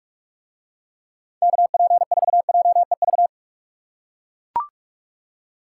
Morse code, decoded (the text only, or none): KP4JEV